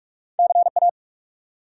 Morse code, heard as a keyed tone, about 44 wpm, 700 Hz. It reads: CA
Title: US States and Canadian Provinces 45wpm